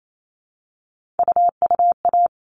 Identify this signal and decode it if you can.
Morse code, heard as a keyed tone, about 28 wpm, 700 Hz. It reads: UUA
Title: All Cut Numbers - 3-Digit 28wpm